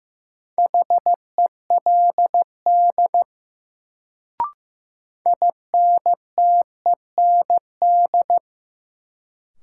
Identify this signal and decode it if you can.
Morse code, heard as a keyed tone, about 15 wpm, 700 Hz. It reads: HELD INTEND
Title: Sentences from Top 1600 Words - New Word Drill - 15wpm